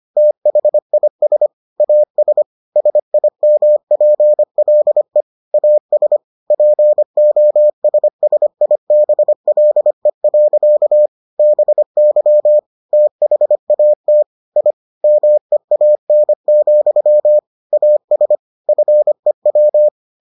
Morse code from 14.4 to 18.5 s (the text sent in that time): I MEAN, AS